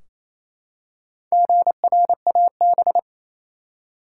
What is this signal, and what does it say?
Morse code, heard as a keyed tone, about 28 wpm, 700 Hz. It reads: GRAB
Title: Four Letter Words - 28wpm